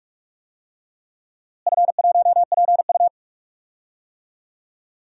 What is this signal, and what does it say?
Morse code, heard as a keyed tone, about 47 wpm, 700 Hz. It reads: F1PU
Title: DXCC-100 Call Signs - 4 Characters - Repeated 3x - 45wpm